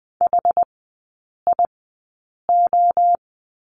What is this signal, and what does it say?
Morse code, heard as a keyed tone, about 20 wpm, spaced as for 8 wpm, 700 Hz. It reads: HIO
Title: 081 Lesson 14e - Number 2 - Three Characters 20wpm